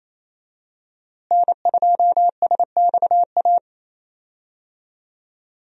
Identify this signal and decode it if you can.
Morse code, heard as a keyed tone, about 28 wpm, 700 Hz. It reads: N2SXA
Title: USA Call Signs - 5 Characters - Repeated 2x - 28wpm